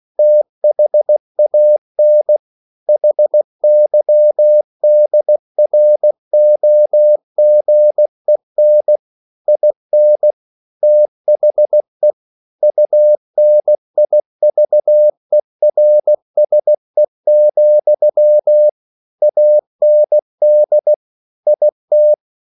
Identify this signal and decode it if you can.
Morse code, heard as a keyed tone, about 16 wpm, 600 Hz. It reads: THAN HYDROGEN IN THE UNIVERSE, AND IT
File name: CW / Practice / EnglishQuotations0216